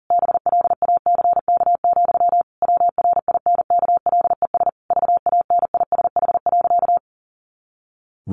Morse code, heard as a keyed tone, about 37 wpm, 700 Hz. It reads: BLACK, WRINKLES VANISH.
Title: A Sound Of Thunder 40wpm